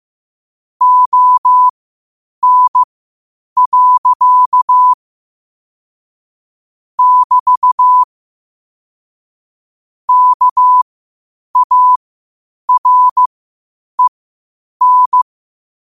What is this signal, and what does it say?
Morse code, heard as a keyed tone, about 15 wpm, spaced as for 8 wpm, 1000 Hz. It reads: ON. <BT> KAREN
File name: qotdfiles / 8wpm